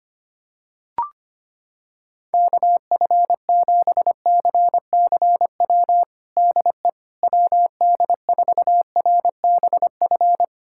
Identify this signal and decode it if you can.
Morse code, heard as a keyed tone, about 25 wpm, 700 Hz. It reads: KF7CCW DE WD4RBF
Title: USA Callsign DE Callsign - 6 Characters - 25wpm